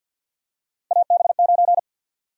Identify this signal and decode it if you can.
Morse code, heard as a keyed tone, about 50 wpm, 700 Hz. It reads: AB9
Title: USA Call Sign Prefixes - 50wpm